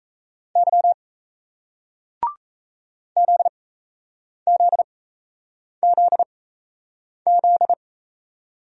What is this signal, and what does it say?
Morse code, heard as a keyed tone, about 36 wpm, spaced as for 8 wpm, 700 Hz. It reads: Y ZZZZ